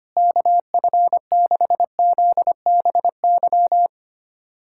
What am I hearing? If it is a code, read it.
Morse code, heard as a keyed tone, about 25 wpm, 700 Hz. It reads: KF6ZBY
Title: USA Callsign DE Callsign - 6 Characters - 25wpm